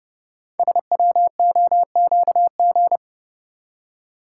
Morse code, heard as a keyed tone, about 30 wpm, 700 Hz. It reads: SWOQG
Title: Code Groups - 5 Letters - 30wpm